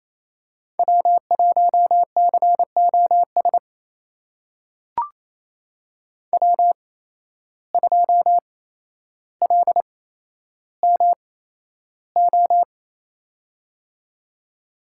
Morse code, heard as a keyed tone, about 28 wpm, spaced as for 12 wpm, 700 Hz. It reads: W1COS W2LMO
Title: USA Call Signs - 5 Characters - 8x Character Spacing - 28wpm